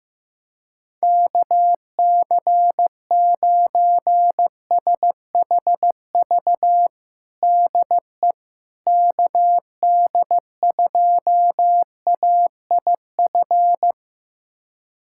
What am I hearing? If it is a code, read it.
Morse code, heard as a keyed tone, about 15 wpm, 700 Hz. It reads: KC9SHV DE KD2AIF
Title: USA Callsign DE Callsign - 6 Characters - 15wpm